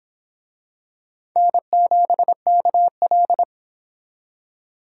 Morse code, heard as a keyed tone, about 25 wpm, 700 Hz. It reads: N7KL